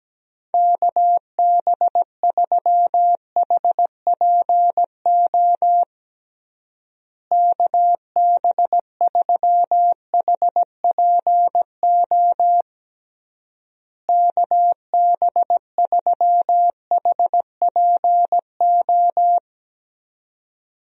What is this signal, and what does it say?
Morse code, heard as a keyed tone, about 17 wpm, 700 Hz. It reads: KB3HPO KB3HPO KB3HPO
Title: USA Call Signs - 6 Characters - Repeated 3x - 17wpm